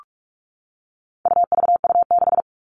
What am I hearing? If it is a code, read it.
Morse code, heard as a keyed tone, about 44 wpm, 700 Hz. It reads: UVU6
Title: All Cut Numbers - 4-Digit 45wpm